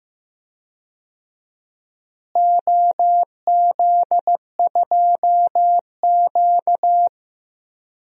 Morse code, heard as a keyed tone, about 15 wpm, 700 Hz. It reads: OZ2Q
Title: DXCC-100 Call Signs - 4 Characters - Repeated 3x - 15wpm